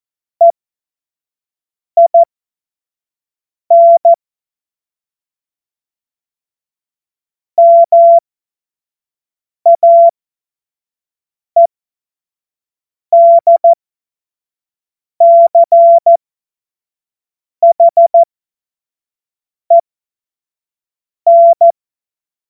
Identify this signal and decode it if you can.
Morse code, heard as a keyed tone, about 14 wpm, spaced as for 5 wpm, 680 Hz. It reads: EIN MAEDCHEN
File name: Morsekurs / Witze / Witz 18. Januar 2022, 25 F70